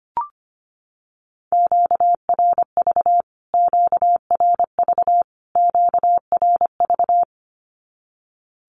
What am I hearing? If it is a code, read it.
Morse code, heard as a keyed tone, about 25 wpm, 700 Hz. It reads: QRV QRV QRV
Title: Q-Codes Repeated 3x 25wpm